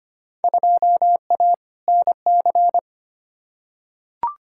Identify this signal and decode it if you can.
Morse code, heard as a keyed tone, about 25 wpm, 700 Hz. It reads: 2A NC
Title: Field Day Exchanges Encore - 25wpm